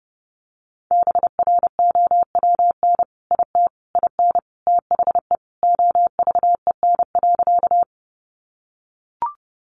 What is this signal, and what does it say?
Morse code, heard as a keyed tone, about 30 wpm, 700 Hz. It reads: BROWN IT IN THE OVEN.
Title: Sentences from Top 1200 Words 30wpm